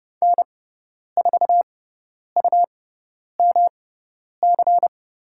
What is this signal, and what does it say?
Morse code, heard as a keyed tone, about 30 wpm, spaced as for 10 wpm, 700 Hz. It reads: N4UMC
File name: Lesson 18g - Letter m - Call Signs 30wpm